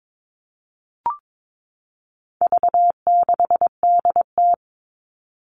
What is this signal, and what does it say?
Morse code, heard as a keyed tone, about 22 wpm, 700 Hz. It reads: V6DT
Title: All Cut Numbers - 4-Digit 22wpm